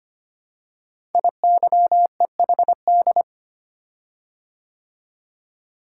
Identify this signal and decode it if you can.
Morse code, heard as a keyed tone, about 25 wpm, 700 Hz. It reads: IYEHD
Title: Code Groups - 5 Letters - 25wpm